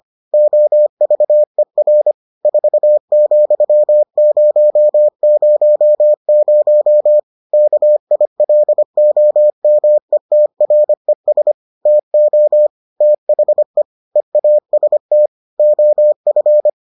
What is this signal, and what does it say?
Morse code, heard as a keyed tone, about 25 wpm, 600 Hz. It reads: OVER 4,000 KILOMETRES TO THE EAST OF